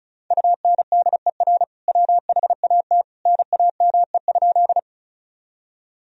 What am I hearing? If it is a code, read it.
Morse code, heard as a keyed tone, about 36 wpm, 700 Hz. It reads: UNDER WHAT NAME?
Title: Sentences from Top 1000 Words - Review Part 1 - 35wpm